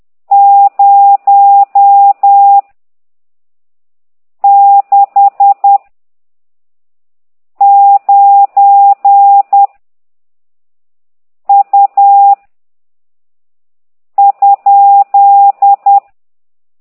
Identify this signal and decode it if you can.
Morse code, heard as a keyed tone, about 10 wpm, 800 Hz. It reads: 0 6 9 U ?